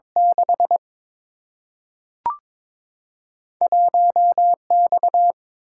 Morse code, heard as a keyed tone, about 22 wpm, 700 Hz. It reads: 6 1X
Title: Single-Letter Single-Number 22wpm